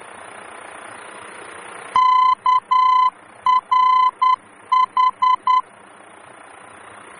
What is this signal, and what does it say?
Morse code, heard as a keyed tone, about 10 wpm, 1020 Hz. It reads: KRH
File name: germany / mx / KRH-D-VOR